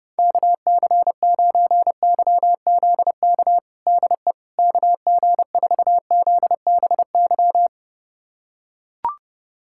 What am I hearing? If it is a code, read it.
Morse code, heard as a keyed tone, about 32 wpm, 700 Hz. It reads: KC9YZK DE KG4ZBY
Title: USA Callsign DE Callsign 30wpm